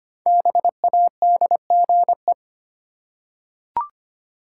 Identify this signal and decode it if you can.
Morse code, heard as a keyed tone, about 25 wpm, 700 Hz. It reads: BADGE
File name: 5 Letter Words - First 3 Letters Given - 25wpm